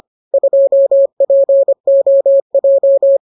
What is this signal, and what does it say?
Morse code, heard as a keyed tone, about 25 wpm, 550 Hz. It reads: 2POJ